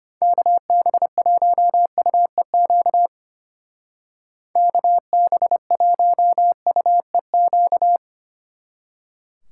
Morse code, heard as a keyed tone, about 32 wpm, 700 Hz. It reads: KB1UEQ KB1UEQ